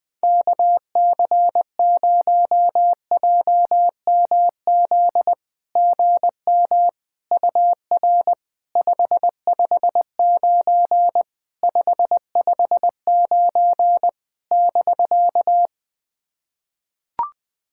Morse code, heard as a keyed tone, about 20 wpm, 690 Hz. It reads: KC0JMZ GM UR 559 559 <BK>
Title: SOTA QSO Practice 20wpm